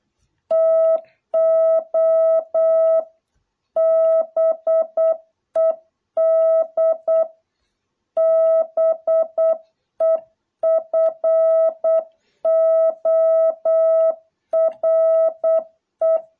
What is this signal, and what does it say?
Morse code, heard as a keyed tone, about 8 wpm, 660 Hz. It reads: TO BED BEFORE